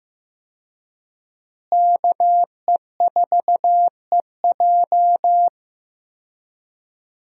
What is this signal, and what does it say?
Morse code, heard as a keyed tone, about 15 wpm, 700 Hz. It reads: KE4EJ
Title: USA Call Signs - 5 Characters - Repeated 3x - 15wpm